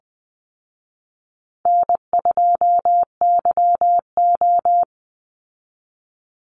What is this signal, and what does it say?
Morse code, heard as a keyed tone, about 20 wpm, 700 Hz. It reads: N2YO
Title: Easy Call Signs Repeated 3x - 20wpm